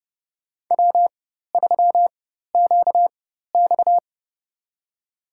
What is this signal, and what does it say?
Morse code, heard as a keyed tone, about 32 wpm, spaced as for 14 wpm, 700 Hz. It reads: W3QX
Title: USA Call Signs - 4x Character Spacing - 30wpm